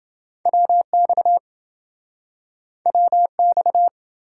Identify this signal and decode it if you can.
Morse code, heard as a keyed tone, about 27 wpm, 700 Hz. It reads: WX WX